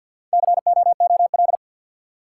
Morse code, heard as a keyed tone, about 52 wpm, 700 Hz. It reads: COOL
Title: Sentences from Top 1200 Words - New Word Drill - 50wpm